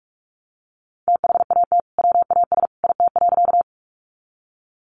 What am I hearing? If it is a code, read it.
Morse code, heard as a keyed tone, about 44 wpm, 700 Hz. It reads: THAT WAS IT.
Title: Sentences from Top 100 Words 45wpm